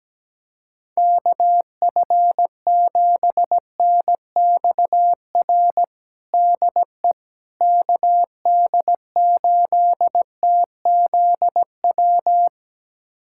Morse code, heard as a keyed tone, about 17 wpm, 700 Hz. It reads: KF7NXR DE KD8TZW
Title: USA Callsign DE Callsign - 6 Characters - 17wpm